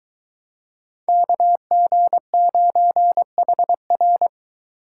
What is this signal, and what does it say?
Morse code, heard as a keyed tone, about 24 wpm, 700 Hz. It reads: KG9HR